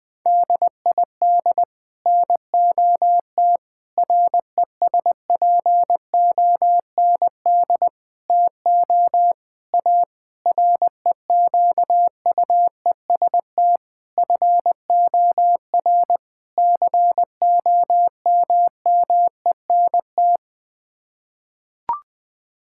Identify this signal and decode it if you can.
Morse code, heard as a keyed tone, about 21 wpm, 700 Hz. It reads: DID NOT RESPOND TO A REQUEST FOR COMMENT
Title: Sets of 8 Words - 20wpm